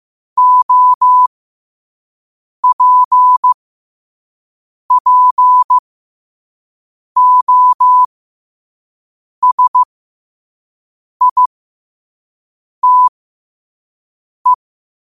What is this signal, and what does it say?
Morse code, heard as a keyed tone, about 15 wpm, spaced as for 5 wpm, 1000 Hz. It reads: OPPOSITE
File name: qotdfiles / 5wpm